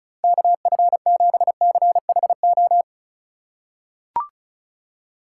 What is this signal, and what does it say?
Morse code, heard as a keyed tone, about 34 wpm, 700 Hz. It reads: KF7CHO
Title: USA Call Signs - 6 Characters - 2x Character Spacing - 35wpm